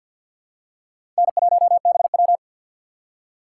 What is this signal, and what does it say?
Morse code, heard as a keyed tone, about 53 wpm, 700 Hz. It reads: N1BW
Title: USA Call Signs - 4 Characters - 3x Character Spacing - 50wpm